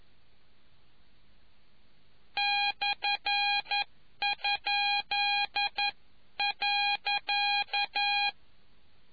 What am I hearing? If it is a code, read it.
Morse code, heard as a keyed tone, about 11 wpm, spaced as for 9 wpm, 820 Hz. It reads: /?.